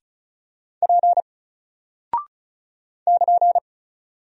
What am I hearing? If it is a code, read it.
Morse code, heard as a keyed tone, about 36 wpm, 700 Hz. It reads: P <KN>
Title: Single Character (Letter:Number:Punctuation:Prosign) 35wpm